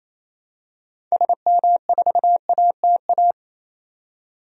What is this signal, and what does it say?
Morse code, heard as a keyed tone, about 28 wpm, 700 Hz. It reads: SM4ATA
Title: DXCC-100 Call Signs - 6 Characters - 28wpm